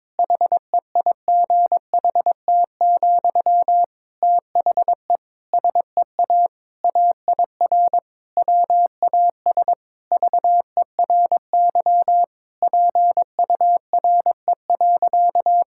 Morse code, heard as a keyed tone, about 22 wpm, 700 Hz. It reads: HEIGHT, THE SEA AIR WAS VERY PURE.
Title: Sentences from Top 1600 Words 22wpm